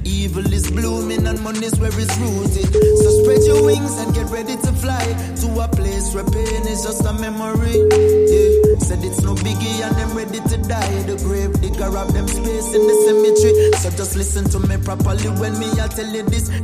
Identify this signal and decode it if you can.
Ringback tone (Europe, 425 Hz), with noise.